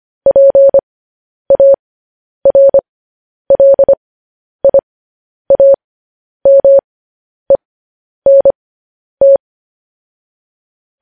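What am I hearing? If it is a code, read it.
Morse code, heard as a keyed tone, about 25 wpm, spaced as for 10 wpm, 550 Hz. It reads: PARLIAMENT
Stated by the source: Morse code practice sending news headlines at 10 wpm.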